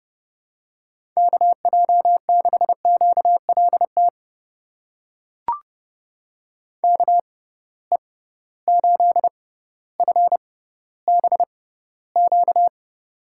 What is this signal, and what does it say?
Morse code, heard as a keyed tone, about 30 wpm, spaced as for 15 wpm, 700 Hz. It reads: KJ6QLT KE8FBQ
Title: USA Call Signs - 6x Character Spacing - 30wpm